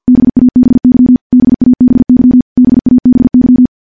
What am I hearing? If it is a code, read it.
Morse code, heard as a keyed tone, about 50 wpm, 250 Hz. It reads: BABY BABY BABY
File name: morse / morse-training-50WPM